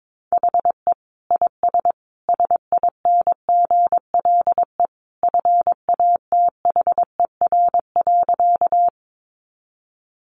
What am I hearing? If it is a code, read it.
Morse code, heard as a keyed tone, about 22 wpm, 700 Hz. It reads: HE IS SINGLE FATHER.
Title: Sentences from Top 700 Words 22wpm